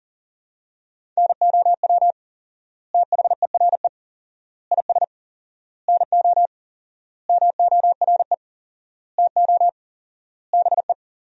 Morse code, heard as a keyed tone, about 37 wpm, 700 Hz. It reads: NOW THERE IS NO MORE TO BE
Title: 4x Word-Spacing - Sentences from Top 100 Words - 40wpm